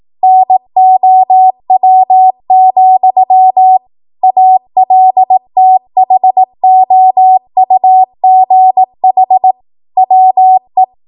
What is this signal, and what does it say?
Morse code, heard as a keyed tone, about 18 wpm, 750 Hz. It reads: NOW, ALTHOUGH WE
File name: Archive / 18 WPM / 18WPM